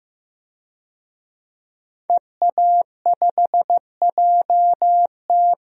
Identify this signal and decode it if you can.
Morse code, heard as a keyed tone, about 14 wpm, 700 Hz. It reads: EA5JT
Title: DXCC-100 Call Signs - 5 Characters - Repeated 3x - 15wpm